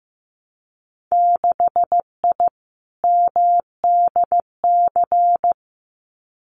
Morse code, heard as a keyed tone, about 15 wpm, 700 Hz. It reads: 6I MDC
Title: Winter Field Day Exchanges Encore 15wpm